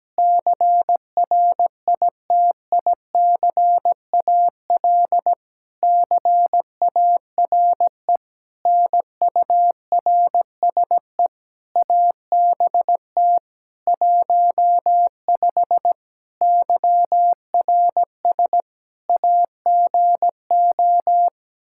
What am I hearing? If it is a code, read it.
Morse code, heard as a keyed tone, about 17 wpm, 700 Hz. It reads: CRITICAL CARE NURSE ABT 15 YRS AGO
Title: Practice QSO - Exchange 3 - More Info - 17wpm